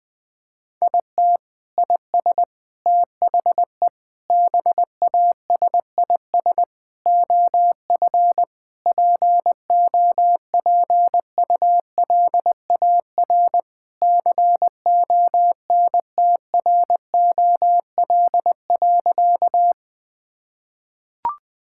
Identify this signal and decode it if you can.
Morse code, heard as a keyed tone, about 20 wpm, 700 Hz. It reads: IT IS THE BASIS OF POPULAR CONTROL.
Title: Sentences from Top 1000 Words - Review Part 5 - 20wpm